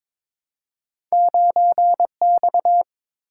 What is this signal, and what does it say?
Morse code, heard as a keyed tone, about 22 wpm, 700 Hz. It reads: 9X
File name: Two Letter-Number Combinations - 22wpm